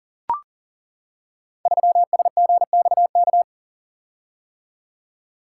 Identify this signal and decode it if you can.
Morse code, heard as a keyed tone, about 40 wpm, 700 Hz. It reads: 3SGXK
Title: Code Groups - 5 Letter:Numbers - 40wpm